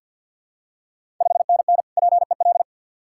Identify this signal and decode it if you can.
Morse code, heard as a keyed tone, about 52 wpm, 690 Hz. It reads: 5NN PEL
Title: Ontario QSO Party Exchange 50wpm